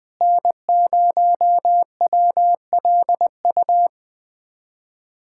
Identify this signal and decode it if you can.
Morse code, heard as a keyed tone, about 20 wpm, 700 Hz. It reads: N0WLU
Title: USA Callsign DE Callsign - 5 Characters - 20wpm